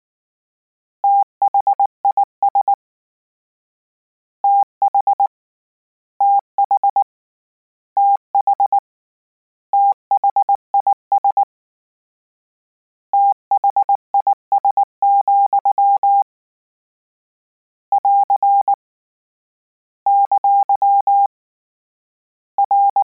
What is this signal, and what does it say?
Morse code, heard as a keyed tone, about 19 wpm, 800 Hz. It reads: THIS TH TH TH THIS THIS, <AR> ! R